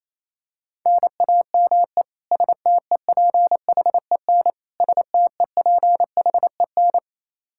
Morse code, heard as a keyed tone, about 28 wpm, 700 Hz. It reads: NAME STEPHEN STEPHEN
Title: Practice QSO - Exchange 1 - Beginning - 28wpm